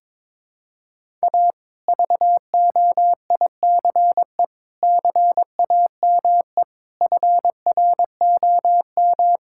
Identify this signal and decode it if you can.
Morse code, heard as a keyed tone, about 21 wpm, 700 Hz. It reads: A VOICE CAME FROM
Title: Sentences from Top 1000 Words - Review Part 5 - 22wpm